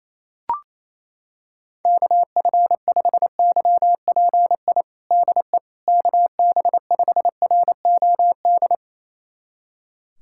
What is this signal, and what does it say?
Morse code, heard as a keyed tone, about 28 wpm, 700 Hz. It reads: KF5YPI DE KB5ROD
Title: USA Callsign DE Callsign 28wpm